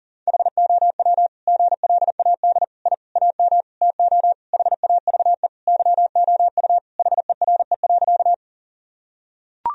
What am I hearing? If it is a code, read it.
Morse code, heard as a keyed tone, about 40 wpm, 700 Hz. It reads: HOW GLAD I AM TO HAVE YOU HERE.
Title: Sentences from Top 1000 Words - Review Part 2 - 40wpm